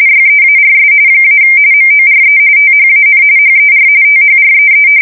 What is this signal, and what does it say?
RTTY, about 45.45 baud. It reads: YEARS - MORE THAN HER THREE
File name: fsk170 / battleshipnj